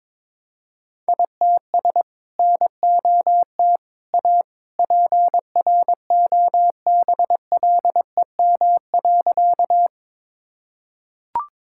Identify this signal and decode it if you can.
Morse code, heard as a keyed tone, about 22 wpm, 700 Hz. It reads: ITS NOT A PROBLEM.
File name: practice / 22wpm / Sentences with 2-Word Phrases - 22wpm